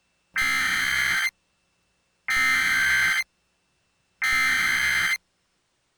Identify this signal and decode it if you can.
Emergency Alert System (SAME) tones (header bursts), heard as harsh buzzing bursts. It is heard from 0.3 to 5.2 s.